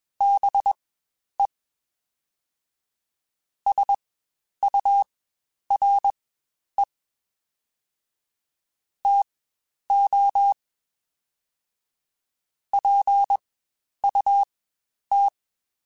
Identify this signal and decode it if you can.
Morse code, heard as a keyed tone, about 21 wpm, spaced as for 10 wpm, 800 Hz. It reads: BE SURE TO PUT
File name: Spaced / Chuckles Spaced-mp3s / BrainMouth sp @20